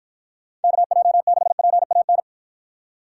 Morse code, heard as a keyed tone, about 49 wpm, 700 Hz. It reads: KJ6PAN